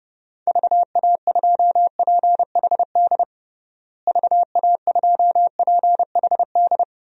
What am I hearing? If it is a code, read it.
Morse code, heard as a keyed tone, about 31 wpm, 700 Hz. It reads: VA2PHD VA2PHD
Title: DXCC-100 Call Signs - 6 Characters - Repeated 2x - 30wpm